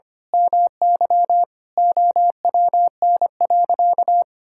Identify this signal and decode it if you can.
Morse code, heard as a keyed tone, about 25 wpm, 700 Hz. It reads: MY OWN.
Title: Sentences from Top 1000 Words - Review Part 6 - 25wpm